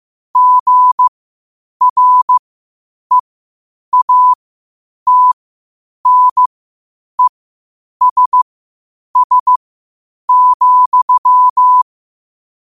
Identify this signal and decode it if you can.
Morse code, heard as a keyed tone, about 15 wpm, spaced as for 8 wpm, 1000 Hz. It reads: GREATNESS,
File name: qotdfiles / 8wpm